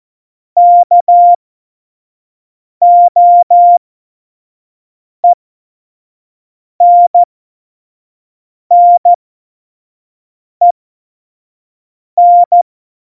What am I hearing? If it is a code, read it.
Morse code, heard as a keyed tone, about 14 wpm, spaced as for 5 wpm, 700 Hz. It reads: KOENNEN
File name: Tempo 25 F70